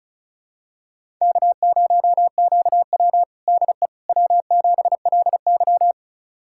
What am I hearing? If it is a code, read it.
Morse code, heard as a keyed tone, about 34 wpm, 700 Hz. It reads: K0QW DE W7LY
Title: USA Callsign DE Callsign - 4 Characters - 35wpm